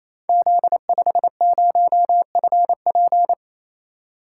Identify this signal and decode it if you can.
Morse code, heard as a keyed tone, about 28 wpm, 700 Hz. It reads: Z50FP
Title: Code Groups - 5 Letter:Numbers - 28wpm